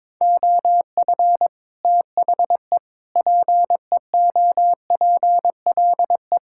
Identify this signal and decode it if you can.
Morse code, heard as a keyed tone, about 22 wpm, 700 Hz. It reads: OF THE PEOPLE
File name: practice / 22wpm / Sentences with 8-Word Phrases - 22wpm